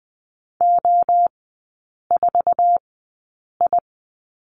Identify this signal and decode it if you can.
Morse code, heard as a keyed tone, about 20 wpm, spaced as for 8 wpm, 700 Hz. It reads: O4I
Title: 047 Lesson 9e - Number 4 - Three Characters 20wpm